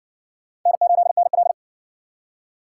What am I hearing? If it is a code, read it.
Morse code, heard as a keyed tone, about 58 wpm, 700 Hz. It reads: N8NL